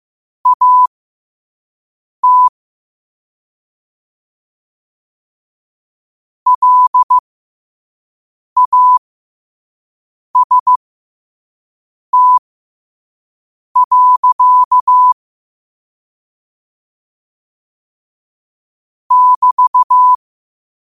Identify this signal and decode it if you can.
Morse code, heard as a keyed tone, about 15 wpm, spaced as for 5 wpm, 1000 Hz. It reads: AT LAST. <BT>
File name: qotdfiles / 5wpm